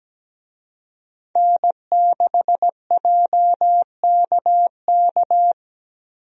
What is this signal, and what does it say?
Morse code, heard as a keyed tone, about 17 wpm, 700 Hz. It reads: N6JKK
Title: USA Call Signs - 5 Characters - Repeated 3x - 17wpm